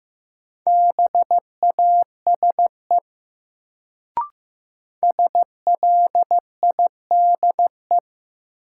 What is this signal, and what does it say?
Morse code, heard as a keyed tone, about 15 wpm, 700 Hz. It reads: BASE SLIDE
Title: Sentences from Top 1600 Words - New Word Drill Alt - 15wpm